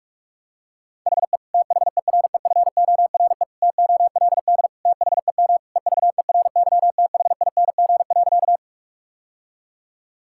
Morse code, heard as a keyed tone, about 41 wpm, 690 Hz. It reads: HE THEREFORE TOLD THEM EVERYTHING.
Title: Sentences from Top 1000 Words - Review Part 5 - 45wpm